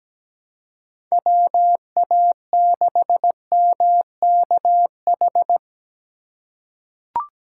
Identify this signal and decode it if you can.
Morse code, heard as a keyed tone, about 17 wpm, 700 Hz. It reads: WA6MKH
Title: USA Call Signs - 4x Character Spacing - 17wpm